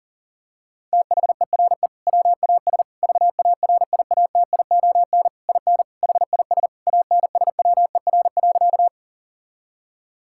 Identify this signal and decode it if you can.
Morse code, heard as a keyed tone, about 40 wpm, 700 Hz. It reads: THERE WAS VARIATION IN HIS ANSWER.
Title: Sentences from Top 1600 Words 40wpm